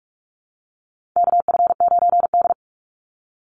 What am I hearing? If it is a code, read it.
Morse code, heard as a keyed tone, about 44 wpm, 700 Hz. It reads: KF9D
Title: Call Signs Repeated 2x - 45wpm